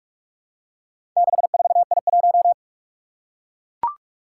Morse code, heard as a keyed tone, about 44 wpm, 700 Hz. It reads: 64I1
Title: Four Challenging Letter-Number Combinations 45wpm